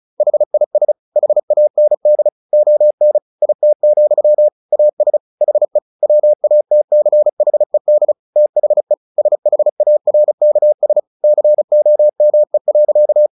Morse code, heard as a keyed tone, about 34 wpm, 600 Hz. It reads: HIS HAND ON IT, AS HE WATCHED THE SHARKS COME.
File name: the old man and the sea 2 35wpm